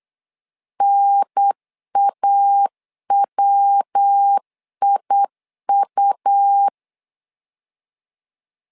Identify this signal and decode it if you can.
Morse code, heard as a keyed tone, about 8 wpm, 800 Hz. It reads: NAWIU
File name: LEZIONI / 4 Q1 8wpm 3035